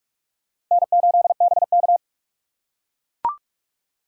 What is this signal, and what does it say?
Morse code, heard as a keyed tone, about 43 wpm, 700 Hz. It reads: N8BK
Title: USA Call Signs - 4 Characters - 6x Character Spacing - 45wpm